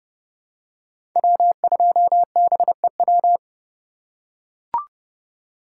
Morse code, heard as a keyed tone, about 30 wpm, 700 Hz. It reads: W2BEW
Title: USA Call Signs - 5 Characters - 4x Character Spacing - 30wpm